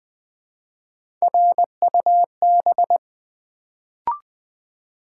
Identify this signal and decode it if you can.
Morse code, heard as a keyed tone, about 20 wpm, 700 Hz. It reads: RUB
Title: Build A Word - 1 Letter At A Time - 3-Letter Words - 20wpm